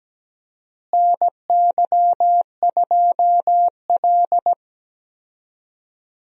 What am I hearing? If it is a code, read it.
Morse code, heard as a keyed tone, about 17 wpm, 700 Hz. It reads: NY2L
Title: USA Call Signs - 4 Characters - Repeated 3x - 17wpm